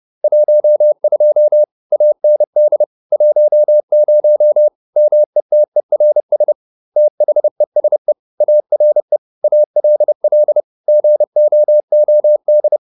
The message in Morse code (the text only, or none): 12 AND 10 METERS THESE ARE ALL GOOD